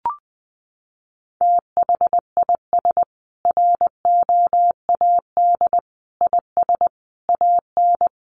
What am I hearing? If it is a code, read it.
Morse code, heard as a keyed tone, about 20 wpm, 700 Hz. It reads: THIS ROAD IS AN
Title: Sentences from Top 1400 Words 20wpm